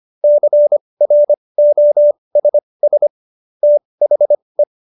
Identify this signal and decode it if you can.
Morse code, heard as a keyed tone, about 25 wpm, spaced as for 20 wpm, 590 Hz. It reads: CROSS THE
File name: Jokes-25-char-20-eff